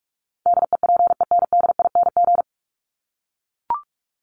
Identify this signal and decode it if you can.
Morse code, heard as a keyed tone, about 44 wpm, 700 Hz. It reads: DEPENDING
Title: ING Suffix - Top 100 Words with Base Words 45wpm